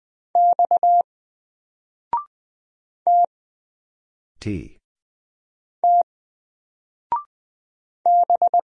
Morse code, heard as a keyed tone, about 20 wpm, 700 Hz. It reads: X T T B
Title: Single Letter-Number Alt 20wpm